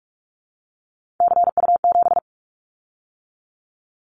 Morse code, heard as a keyed tone, about 44 wpm, 700 Hz. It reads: CU7
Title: Callsign Prefixes for Top 100 DXCC - 45wpm